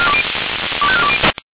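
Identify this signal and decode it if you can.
Five-tone selective call (ZVEI), heard as quick runs of five tones, tones of about 70 ms. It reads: 25420
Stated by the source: ZVEI five-tone selective call received on SDR.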